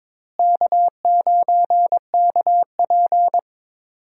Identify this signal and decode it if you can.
Morse code, heard as a keyed tone, about 22 wpm, 700 Hz. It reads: K9KP
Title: USA Call Signs - 4 Characters - Repeated 2x - 22wpm